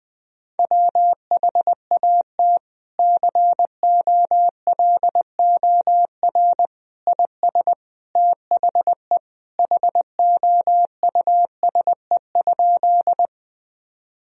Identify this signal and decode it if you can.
Morse code, heard as a keyed tone, about 20 wpm, 700 Hz. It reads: WHAT COLOR IS THE HOUSE?
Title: Sentences from Top 1000 Words - Review Part 2 - 20wpm